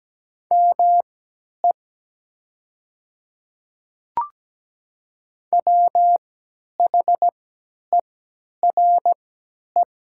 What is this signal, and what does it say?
Morse code, heard as a keyed tone, about 17 wpm, spaced as for 10 wpm, 700 Hz. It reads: ME WHERE
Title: 3x Character Spacing - Top 500 Words - 17wpm